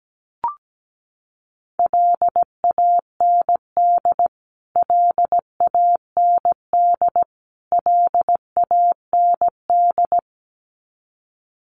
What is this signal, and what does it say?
Morse code, heard as a keyed tone, about 17 wpm, 700 Hz. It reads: LAND LAND LAND
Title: Top 300 Words Repeated 17wpm